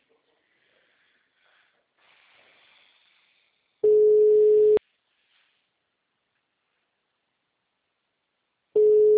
Ringback tone (Europe, 425 Hz).
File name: out- -2012- - - .69878